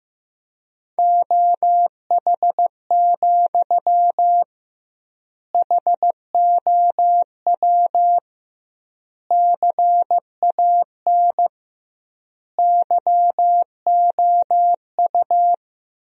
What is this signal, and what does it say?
Morse code, heard as a keyed tone, about 15 wpm, 700 Hz. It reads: OH, HOW CAN YOU
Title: 2x Word-Spacing - Sentences from Top 500 Words - 15wpm